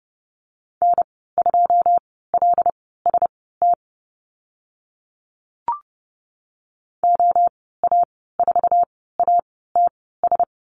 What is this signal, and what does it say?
Morse code, heard as a keyed tone, about 30 wpm, spaced as for 17 wpm, 700 Hz. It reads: N2LST OA4ATS
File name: Lesson 14g - Number 2 - Call Signs 30wpm